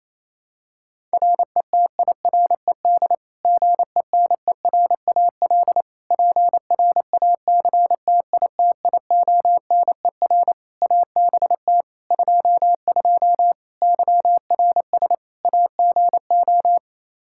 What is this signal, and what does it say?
Morse code, heard as a keyed tone, about 28 wpm, 700 Hz. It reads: RETIRED GENERAL PRACTITIONER ABT 22 YRS AGO
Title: Practice QSO - Exchange 3 - More Info - 28wpm